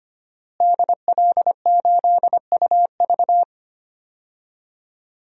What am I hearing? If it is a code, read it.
Morse code, heard as a keyed tone, about 25 wpm, 700 Hz. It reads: DL8UV
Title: DXCC-100 Call Signs - 5 Characters - Repeated 2x - 25wpm